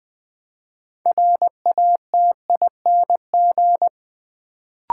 Morse code, heard as a keyed tone, about 20 wpm, 700 Hz. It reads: RATING
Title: Build A Word - 3 Letters At A Time - 6-Letter Words - 20wpm